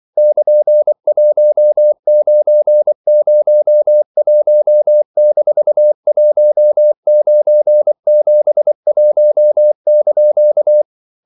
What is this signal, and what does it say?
Morse code, heard as a keyed tone, about 24 wpm, 600 Hz. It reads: <KN>1901-1971)
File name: CW / Practice / EnglishQuotations0424